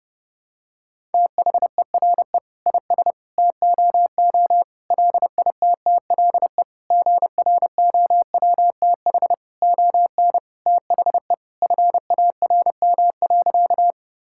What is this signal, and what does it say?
Morse code, heard as a keyed tone, about 29 wpm, 700 Hz. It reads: THERE IS TOO LITTLE GROWTH ON THE FARM.
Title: Sentences from Top 1000 Words - Review Part 5 - 30wpm